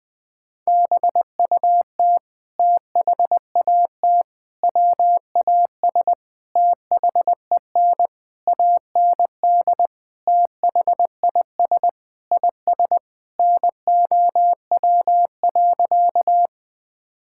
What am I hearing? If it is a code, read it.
Morse code, heard as a keyed tone, about 21 wpm, 700 Hz. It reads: BUT THAT WAS THEN AND THIS IS NOW.
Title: Sentences with 8-Word Phrases - 20wpm